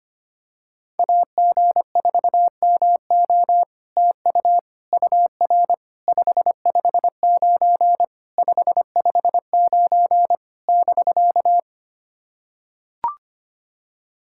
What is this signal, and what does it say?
Morse code, heard as a keyed tone, about 25 wpm, 700 Hz. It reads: AG4MO TU UR 559 559 <BK>
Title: SOTA QSO Practice 25wpm